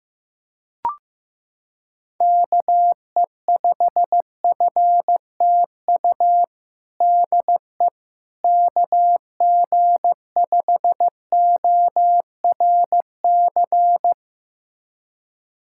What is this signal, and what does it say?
Morse code, heard as a keyed tone, about 15 wpm, 700 Hz. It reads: KE5FTU DE KG5ORC
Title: USA Callsign DE Callsign 15wpm